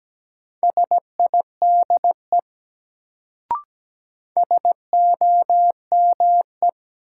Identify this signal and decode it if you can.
Morse code, heard as a keyed tone, about 17 wpm, 700 Hz. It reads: SIDE SOME
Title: Sets of 2 Words - New Word Drill - 17wpm